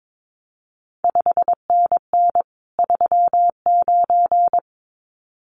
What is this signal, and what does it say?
Morse code, heard as a keyed tone, about 22 wpm, 700 Hz. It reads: 5NN 39
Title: CQ WW DX Contest Exchange Encore - 22wpm